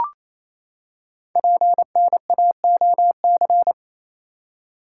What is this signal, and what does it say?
Morse code, heard as a keyed tone, about 28 wpm, 700 Hz. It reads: PNAOC
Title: Code Groups - 5 Letters - 28wpm